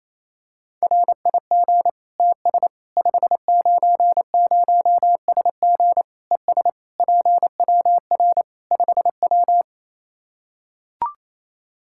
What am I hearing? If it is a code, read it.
Morse code, heard as a keyed tone, about 28 wpm, 700 Hz. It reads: RIG TS 590SG ES PWR 5W
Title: Practice QSO - Exchange 2 - Info - 28wpm